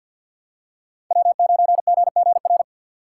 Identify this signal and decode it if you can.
Morse code, heard as a keyed tone, about 50 wpm, 700 Hz. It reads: W9GGR
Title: USA Call Signs - 5 Characters - 3x Character Spacing - 50wpm